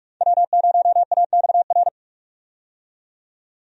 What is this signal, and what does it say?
Morse code, heard as a keyed tone, about 44 wpm, 700 Hz. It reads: W0AXR
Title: USA Call Signs - 5 Characters - 45wpm